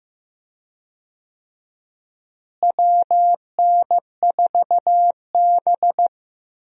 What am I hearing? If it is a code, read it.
Morse code, heard as a keyed tone, about 15 wpm, 700 Hz. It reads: WN4B
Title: USA Call Signs - 4 Characters - Repeated 3x - 15wpm